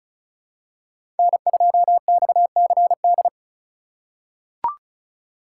Morse code, heard as a keyed tone, about 36 wpm, 700 Hz. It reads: N2XCD
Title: USA Call Signs - 5 Characters - 6x Character Spacing - 35wpm